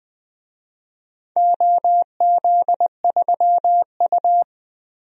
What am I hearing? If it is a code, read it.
Morse code, heard as a keyed tone, about 20 wpm, 700 Hz. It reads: OZ3U
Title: DXCC-100 Call Signs - 4 Characters - 20wpm